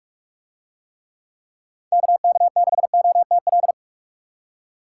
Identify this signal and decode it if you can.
Morse code, heard as a keyed tone, about 41 wpm, 700 Hz. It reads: KK6OTL